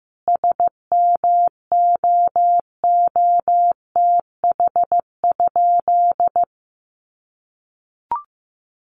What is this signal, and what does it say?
Morse code, heard as a keyed tone, about 15 wpm, 700 Hz. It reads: SMOOTH?
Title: Sentences from Top 1400 Words 15wpm